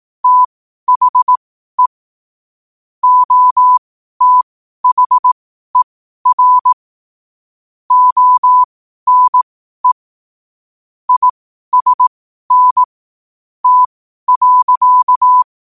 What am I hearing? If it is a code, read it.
Morse code, heard as a keyed tone, about 18 wpm, spaced as for 12 wpm, 1000 Hz. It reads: THE OTHER ONE ISN T.
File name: qotdfiles / 12wpm